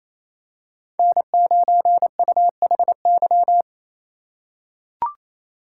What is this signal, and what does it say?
Morse code, heard as a keyed tone, about 28 wpm, 700 Hz. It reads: N9UHY
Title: USA Call Signs - 5 Characters - 4x Character Spacing - 28wpm